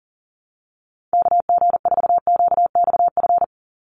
Morse code, heard as a keyed tone, about 40 wpm, 700 Hz. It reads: KG4QXF
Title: USA Call Signs 40wpm